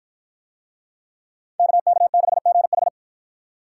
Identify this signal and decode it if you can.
Morse code, heard as a keyed tone, about 50 wpm, 700 Hz. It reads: KK6GH